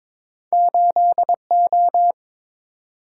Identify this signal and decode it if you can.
Morse code, heard as a keyed tone, about 22 wpm, 700 Hz. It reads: 8O
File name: practice / 22wpm / Two Letter-Number Combinations - 22wpm